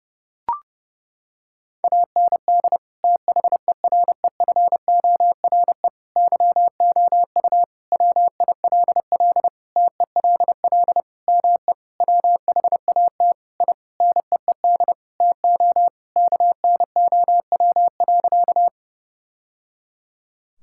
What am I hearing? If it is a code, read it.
Morse code, heard as a keyed tone, about 30 wpm, 700 Hz. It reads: AND THEREFORE YOU WILL TELL ME WHAT I NEED TO KNOW.
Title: Sentences from Top 1000 Words - Review Part 5 - 30wpm